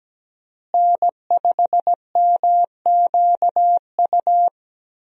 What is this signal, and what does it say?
Morse code, heard as a keyed tone, about 17 wpm, 700 Hz. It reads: N5MQU
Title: USA Call Signs - 5 Characters - Repeated 2x - 17wpm